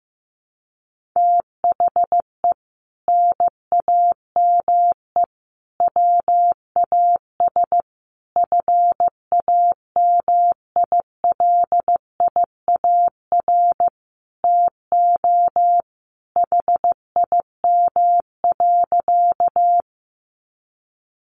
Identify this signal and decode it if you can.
Morse code, heard as a keyed tone, about 15 wpm, 700 Hz. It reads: THE NAME WAS FAMILIAR TO HIM.
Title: Sentences from Top 1000 Words 15wpm